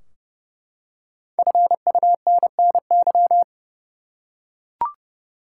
Morse code, heard as a keyed tone, about 32 wpm, 700 Hz. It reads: FUNNY
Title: Five Letter Words - 30wpm